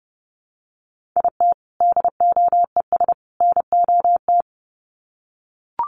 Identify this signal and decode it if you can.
Morse code, heard as a keyed tone, about 30 wpm, 700 Hz. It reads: IT DOES NOT
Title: Sets of 3 Words 30wpm